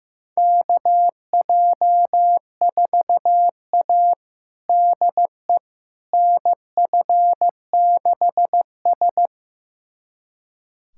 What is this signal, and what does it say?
Morse code, heard as a keyed tone, about 15 wpm, 700 Hz. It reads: KJ4A DE NF6S
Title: USA Callsign DE Callsign - 4 Characters - 15wpm